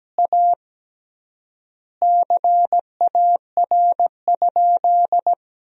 Morse code, heard as a keyed tone, about 17 wpm, 700 Hz. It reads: A CAR?
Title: 3x Word-Spacing - Sentences from Top 300 Words - 17wpm